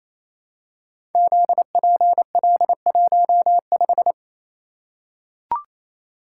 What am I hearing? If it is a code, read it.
Morse code, heard as a keyed tone, about 29 wpm, 700 Hz. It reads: ZPL15
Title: Code Groups - 5 Letter:Numbers - 28wpm